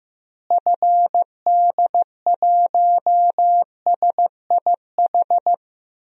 Morse code, heard as a keyed tone, about 15 wpm, 700 Hz. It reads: FD1SIH
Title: DXCC-100 Call Signs - 6 Characters - Repeated 3x - 15wpm